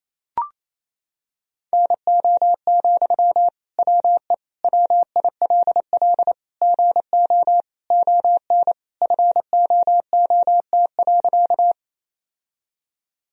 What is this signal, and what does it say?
Morse code, heard as a keyed tone, about 28 wpm, 700 Hz. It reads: NO, WE WILL GO ON FOOT.
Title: Sentences from Top 1000 Words - Review Part 3 - 28wpm